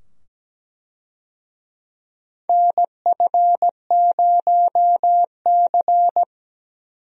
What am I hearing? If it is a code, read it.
Morse code, heard as a keyed tone, about 17 wpm, 700 Hz. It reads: NF0C
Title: USA Call Signs - 4 Characters - 17wpm